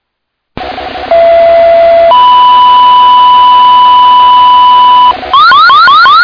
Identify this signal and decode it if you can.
Two-tone page, heard as a one-second tone, then a three-second tone, about 670 then 990 Hz, with noise. It is heard from 1.1 to 5.1 s.